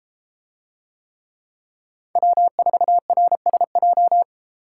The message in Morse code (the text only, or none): W4RSJ